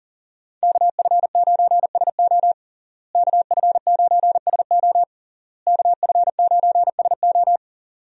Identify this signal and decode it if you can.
Morse code, heard as a keyed tone, about 40 wpm, 700 Hz. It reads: KF9SO KF9SO KF9SO
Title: USA Call Signs - 5 Characters - Repeated 3x - 40wpm